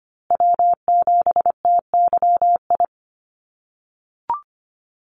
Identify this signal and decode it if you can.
Morse code, heard as a keyed tone, about 25 wpm, 700 Hz. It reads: W7TYI
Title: USA Call Signs Repeated 2x 25wpm